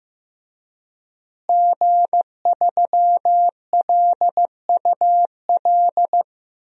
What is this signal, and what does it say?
Morse code, heard as a keyed tone, about 15 wpm, 700 Hz. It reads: G3LUL
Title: DXCC-100 Call Signs - 5 Characters - Repeated 3x - 15wpm